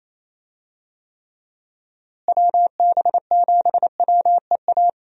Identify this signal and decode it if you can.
Morse code, heard as a keyed tone, about 28 wpm, 700 Hz. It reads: WB7WEA
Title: USA Call Signs - 6 Characters - Repeated 2x - 28wpm